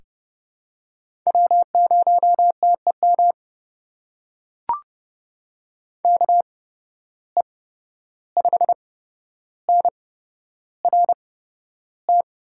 Morse code, heard as a keyed tone, about 30 wpm, spaced as for 12 wpm, 700 Hz. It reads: W0TEM KE5NRT
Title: USA Call Signs - 8x Character Spacing - 30wpm